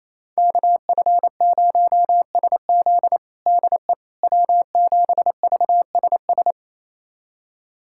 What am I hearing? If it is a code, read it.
Morse code, heard as a keyed tone, about 27 wpm, 700 Hz. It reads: KF0SZ DE W7VSS
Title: USA Callsign DE Callsign - 5 Characters - 28wpm